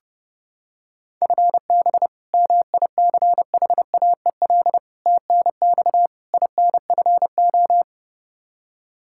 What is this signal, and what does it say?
Morse code, heard as a keyed tone, about 29 wpm, 700 Hz. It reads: FB MICHAEL TNX INFO
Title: Practice QSO - Exchange 3 - More Info - 30wpm